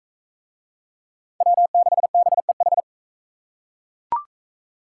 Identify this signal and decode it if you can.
Morse code, heard as a keyed tone, about 40 wpm, 700 Hz. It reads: W6BEH